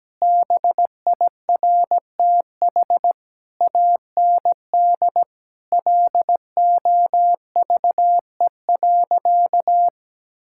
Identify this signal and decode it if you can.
Morse code, heard as a keyed tone, about 17 wpm, 700 Hz. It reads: BIRTH AND LOVE.
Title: Sentences from Top 1000 Words - Review Part 5 - 17wpm